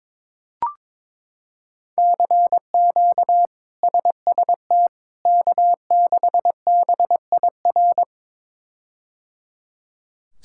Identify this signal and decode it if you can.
Morse code, heard as a keyed tone, about 22 wpm, 700 Hz. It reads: CQ SST K6BIR
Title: K1USN SST QSO Practice - 22wpm